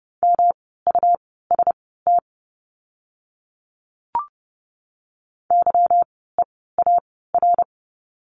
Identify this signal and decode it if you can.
Morse code, heard as a keyed tone, about 30 wpm, spaced as for 17 wpm, 700 Hz. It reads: MUST YEAR
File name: course / 30:10wpm / 237 Lesson 38f - Letter q - Words 30wpm